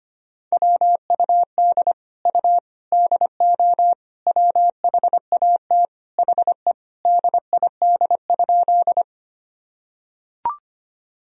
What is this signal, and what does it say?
Morse code, heard as a keyed tone, about 25 wpm, 700 Hz. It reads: WUD U DO WHAT HE DID?
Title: Sentences from Top 100 Words and Abbreviations 25wpm